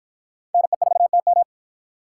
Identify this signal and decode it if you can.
Morse code, heard as a keyed tone, about 53 wpm, 700 Hz. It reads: NE4TM